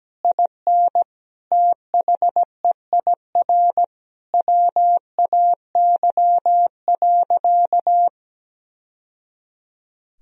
Morse code, heard as a keyed tone, about 17 wpm, 700 Hz. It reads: IN THEIR WAY.
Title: Sentences from Top 1000 Words - Review Part 1 - 17wpm